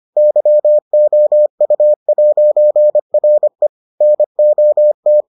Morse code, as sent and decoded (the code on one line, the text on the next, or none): -.-- --- ..- .----. .-. / -. --- -
YOU'RE NOT